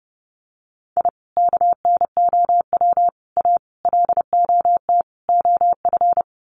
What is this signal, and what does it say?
Morse code, heard as a keyed tone, about 30 wpm, 700 Hz. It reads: I KNOW A LOT OF
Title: Sets of 5 Words Encore 30wpm